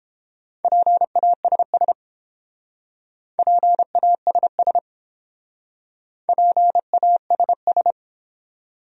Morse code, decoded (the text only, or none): PASS PASS PASS